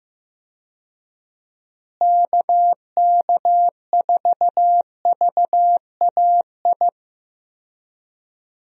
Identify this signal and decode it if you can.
Morse code, heard as a keyed tone, about 15 wpm, 700 Hz. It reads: KK4VAI
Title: USA Call Signs - 6 Characters - Repeated 2x - 15wpm